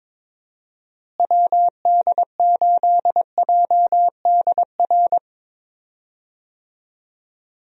Morse code, heard as a keyed tone, about 22 wpm, 700 Hz. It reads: WD8JDR
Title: USA Call Signs - 6 Characters - Repeated 3x - 22wpm